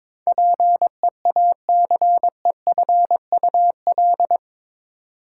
Morse code, heard as a keyed tone, about 22 wpm, 700 Hz. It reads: PEACEFUL
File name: practice / 22wpm / Eight Letter Words - 22wpm